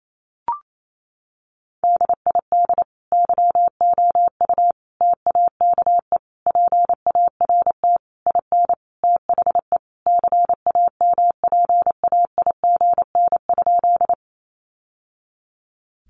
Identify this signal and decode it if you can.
Morse code, heard as a keyed tone, about 28 wpm, 700 Hz. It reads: DID YOU TAKE PART IN THE CAMPAIGN?
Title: Sentences from Top 1200 Words 28wpm